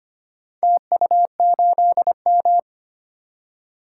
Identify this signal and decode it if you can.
Morse code, heard as a keyed tone, about 25 wpm, 700 Hz. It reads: TU8M
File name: practice / 25wpm / Four Letter-Number Combinations - 25wpm